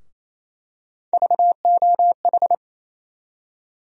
Morse code, heard as a keyed tone, about 28 wpm, 700 Hz. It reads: VOH
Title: USA Call Sign Postfixes - 3 Letters 28wpm